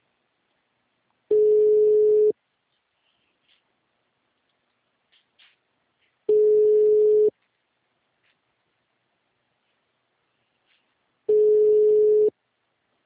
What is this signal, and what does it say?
Ringback tone (Europe, 425 Hz).